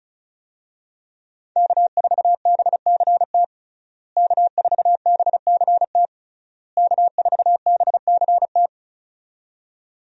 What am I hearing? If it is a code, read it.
Morse code, heard as a keyed tone, about 36 wpm, 700 Hz. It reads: K4BCT K4BCT K4BCT
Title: USA Call Signs - 5 Characters - Repeated 3x - 35wpm